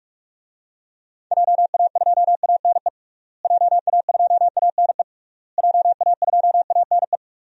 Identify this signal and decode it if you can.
Morse code, heard as a keyed tone, about 47 wpm, 700 Hz. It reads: JA2ANE JA2ANE JA2ANE
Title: DXCC-100 Call Signs - 6 Characters - Repeated 3x - 45wpm